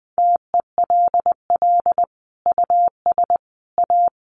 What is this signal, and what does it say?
Morse code, heard as a keyed tone, about 19 wpm, 700 Hz. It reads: TELL US A
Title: Sentences from Top 300 Words 20wpm